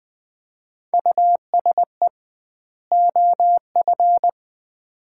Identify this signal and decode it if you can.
Morse code, heard as a keyed tone, about 20 wpm, 700 Hz. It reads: USE OF
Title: 2x Word-Spacing - Sets of 2 Words - 20wpm